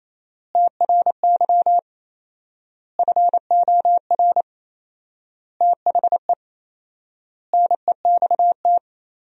Morse code, heard as a keyed tone, about 28 wpm, 700 Hz. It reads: TRY FOR THE NEXT
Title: 4x Word-Spacing - Sentences from Top 400 Words - 28wpm